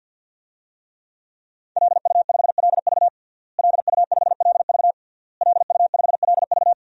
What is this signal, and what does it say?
Morse code, heard as a keyed tone, about 52 wpm, 700 Hz. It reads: LU5LV LU5LV LU5LV
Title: DXCC-100 Call Signs - 5 Characters - Repeated 3x - 50wpm